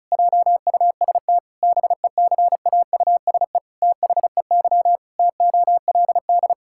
Morse code, heard as a keyed tone, about 36 wpm, 700 Hz. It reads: JUST BECAUSE THEY TOLD
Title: Pangram Challenge v2 - 35wpm